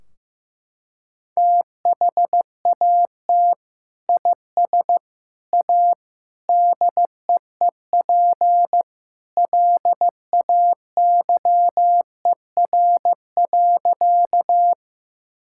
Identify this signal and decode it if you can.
Morse code, heard as a keyed tone, about 15 wpm, 700 Hz. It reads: THAT IS A DEEP LAYER.
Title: Sentences from Top 1600 Words 15wpm